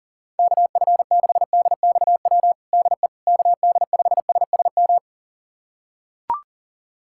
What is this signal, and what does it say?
Morse code, heard as a keyed tone, about 37 wpm, 700 Hz. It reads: KF6DXW DE KD5SSM
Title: USA Callsign DE Callsign - 6 Characters - 40wpm